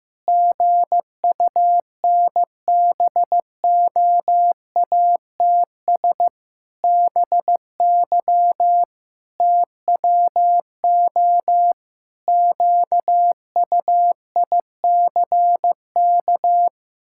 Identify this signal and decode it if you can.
Morse code, heard as a keyed tone, about 15 wpm, 700 Hz. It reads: GUNBOATS BY TWO QUICK
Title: Pangram Challenge v2 - 15wpm